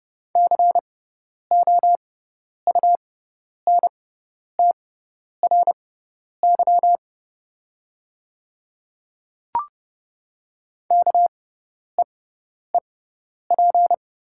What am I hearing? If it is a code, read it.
Morse code, heard as a keyed tone, about 31 wpm, spaced as for 10 wpm, 700 Hz. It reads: COUNTRY KEEP
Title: 6x Character Spacing - Top 300 Words - 30wpm